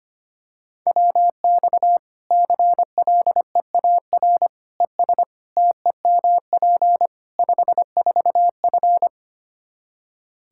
Morse code, heard as a keyed tone, about 25 wpm, 700 Hz. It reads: WX CLEAR ES TEMP 54F
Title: Practice QSO - Exchange 2 - Info - 25wpm